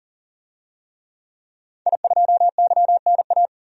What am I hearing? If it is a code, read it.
Morse code, heard as a keyed tone, about 37 wpm, 700 Hz. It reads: I2YNA
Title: DXCC-100 Call Signs - 5 Characters - Repeated 2x - 40wpm